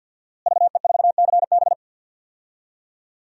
Morse code, heard as a keyed tone, about 46 wpm, 700 Hz. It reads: VE4CB
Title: DXCC-100 Call Signs - 5 Characters - Repeated 3x - 50wpm